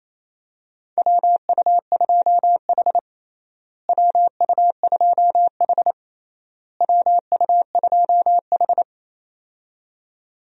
Morse code, heard as a keyed tone, about 28 wpm, 700 Hz. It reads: WU2H WU2H WU2H
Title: USA Call Signs - 4 Characters - Repeated 3x - 28wpm